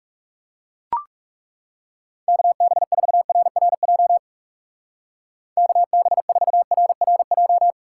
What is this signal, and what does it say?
Morse code, heard as a keyed tone, about 37 wpm, 700 Hz. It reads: KB4RRJ KB4RRJ
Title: Speed Racing - USA 2x3 Call Signs 30wpm